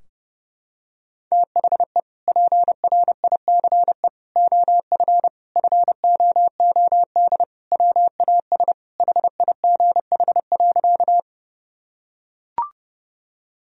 Morse code, heard as a keyed tone, about 30 wpm, 700 Hz. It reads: THE PRICE OF FOOD WAS HIGH.
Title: Sentences from Top 1000 Words - Review Part 4 - 30wpm